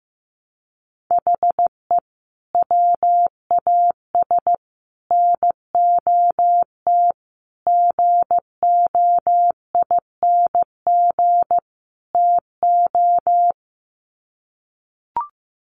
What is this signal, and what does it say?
Morse code, heard as a keyed tone, about 15 wpm, 700 Hz. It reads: HE WAS NOT GOING TO
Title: Sets of 5 Words 15wpm